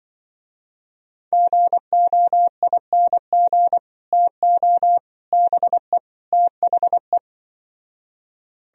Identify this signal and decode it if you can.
Morse code, heard as a keyed tone, about 25 wpm, 700 Hz. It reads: GOING TO BE THE
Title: Speed-Racing - Sets of 4 Words 20wpm